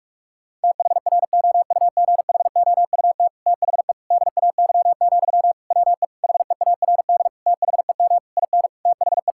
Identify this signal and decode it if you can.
Morse code, heard as a keyed tone, about 46 wpm, 700 Hz. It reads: THROUGHOUT THE DAY, WE HEARD THEM IN THE
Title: Sentences from Top 1000 Words - Review Part 5 - 45wpm